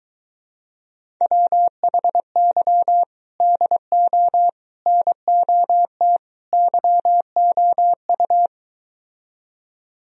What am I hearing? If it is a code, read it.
Morse code, heard as a keyed tone, about 23 wpm, 700 Hz. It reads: WHY DO NOT YOU